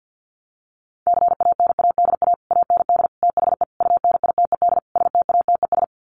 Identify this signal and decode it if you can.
Morse code, heard as a keyed tone, about 50 wpm, 700 Hz. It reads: CANADA AND THE UNITED STATES
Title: Sets of 5 Words Encore 50wpm